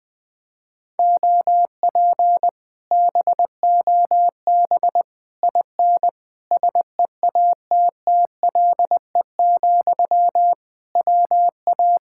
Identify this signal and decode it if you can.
Morse code, heard as a keyed tone, about 20 wpm, 700 Hz. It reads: OP BOB IN SEATTLE, WA
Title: Speed-Racing Common QSO Phrases Encore - 20wpm